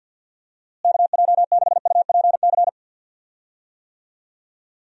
Morse code, heard as a keyed tone, about 48 wpm, 700 Hz. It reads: KJ6UPC
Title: USA Call Signs - 6 Characters - Repeated 2x - 50wpm